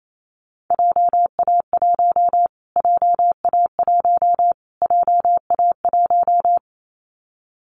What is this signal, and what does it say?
Morse code, heard as a keyed tone, about 28 wpm, 700 Hz. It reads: JA1 JA1 JA1
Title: Callsign Prefixes for Top 25 DXCC - Repeated 3x - 28wpm